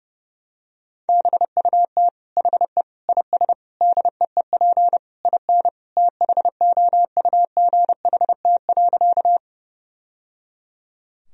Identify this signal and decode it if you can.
Morse code, heard as a keyed tone, about 32 wpm, 700 Hz. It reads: BUT HE IS DEEP IN THOUGHT.
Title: Sentences from Top 1000 Words - Review Part 3 - 30wpm